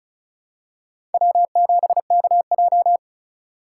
Morse code, heard as a keyed tone, about 36 wpm, 690 Hz. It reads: W7KJ
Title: USA Call Signs - 4 Characters - 3x Character Spacing - 35wpm